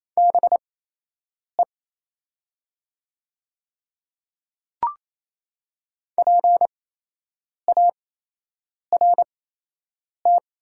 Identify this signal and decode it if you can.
Morse code, heard as a keyed tone, about 28 wpm, spaced as for 8 wpm, 700 Hz. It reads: BE PART
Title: 8x Character Spacing - Top 200 Words - 28wpm